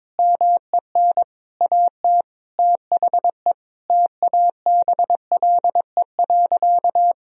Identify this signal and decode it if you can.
Morse code, heard as a keyed tone, about 22 wpm, 700 Hz. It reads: MEN AT THE TABLE.
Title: Sentences from Top 1000 Words - Review Part 2 - 22wpm